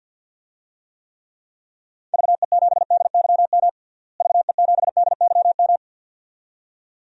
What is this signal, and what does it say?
Morse code, heard as a keyed tone, about 52 wpm, 700 Hz. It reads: VE7DYM VE7DYM
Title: DXCC-100 Call Signs - 6 Characters - Repeated 2x - 50wpm